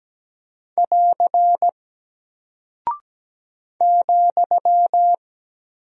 Morse code, heard as a keyed tone, about 17 wpm, 700 Hz. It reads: <AR> ,
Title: Single Character (Letter:Number:Punctuation:Prosign) Alt 17wpm